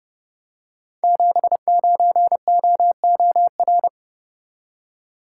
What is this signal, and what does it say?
Morse code, heard as a keyed tone, about 28 wpm, 700 Hz. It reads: 79OOR
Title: Code Groups - 5 Letter:Numbers - 30wpm